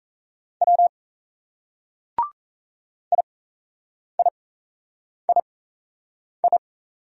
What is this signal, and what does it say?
Morse code, heard as a keyed tone, about 35 wpm, spaced as for 8 wpm, 700 Hz. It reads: W IIII
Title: Speed-Racing Single Letters 28wpm